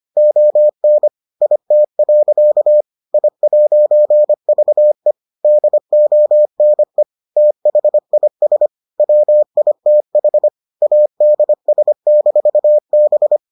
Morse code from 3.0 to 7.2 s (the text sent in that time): I'VE DONE